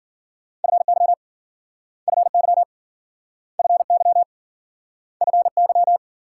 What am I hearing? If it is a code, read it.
Morse code, heard as a keyed tone, about 55 wpm, 700 Hz. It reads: FY FY FY FY